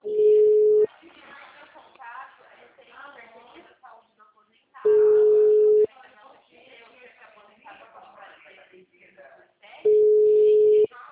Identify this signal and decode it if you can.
Ringback tone (Europe, 425 Hz), with speech.